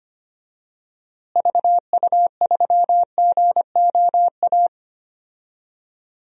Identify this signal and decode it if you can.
Morse code, heard as a keyed tone, about 25 wpm, 700 Hz. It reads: VU3GOA
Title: DXCC-100 Call Signs - 6 Characters - Repeated 2x - 25wpm